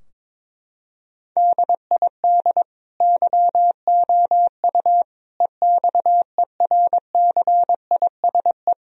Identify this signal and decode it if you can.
Morse code, heard as a keyed tone, about 22 wpm, 700 Hz. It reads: DID YOU EXERCISE
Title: Sentences from Top 1000 Words - Review Part 4 - 22wpm